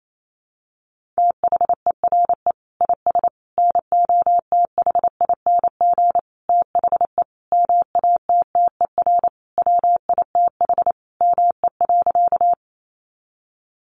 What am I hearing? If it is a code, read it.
Morse code, heard as a keyed tone, about 28 wpm, 700 Hz. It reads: THERE IS NOTHING THE MATTER WITH ME.
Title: Sentences from Top 700 Words 28wpm